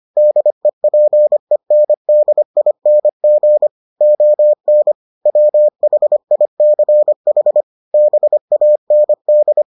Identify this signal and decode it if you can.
Morse code, heard as a keyed tone, about 25 wpm, 600 Hz. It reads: DEPENDING ON WHICH BAND